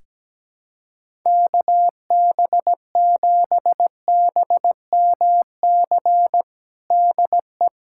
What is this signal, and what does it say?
Morse code, heard as a keyed tone, about 17 wpm, 700 Hz. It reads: KB7BMC DE
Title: USA Callsign DE Callsign - 6 Characters - 17wpm